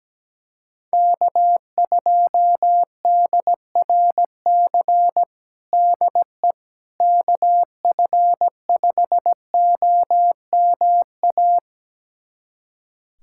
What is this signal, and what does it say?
Morse code, heard as a keyed tone, about 17 wpm, 700 Hz. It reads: K2DRC DE KF5OMA
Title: USA Callsign DE Callsign 17wpm